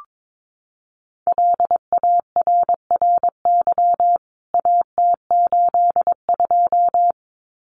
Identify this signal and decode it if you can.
Morse code, heard as a keyed tone, about 22 wpm, 700 Hz. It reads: LARRY AT82
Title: CWT Exchange 22wpm